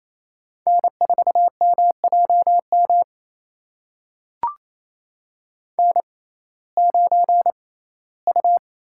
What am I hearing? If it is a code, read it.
Morse code, heard as a keyed tone, about 28 wpm, spaced as for 17 wpm, 700 Hz. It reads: N4MJM N9U
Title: USA Call Signs - 6x Character Spacing - 28wpm